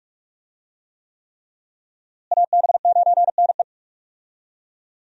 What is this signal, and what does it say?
Morse code, heard as a keyed tone, about 42 wpm, 700 Hz. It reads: AB9NE
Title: USA Call Signs - 5 Characters - Repeated 3x - 45wpm